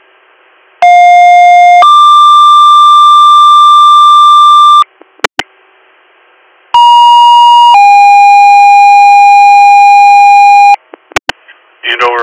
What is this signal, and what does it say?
Two-tone page, heard as a one-second tone, then a three-second tone, about 710 then 1180 Hz, about 950 then 810 Hz, with speech. It is heard from 0.8 to 10.8 s.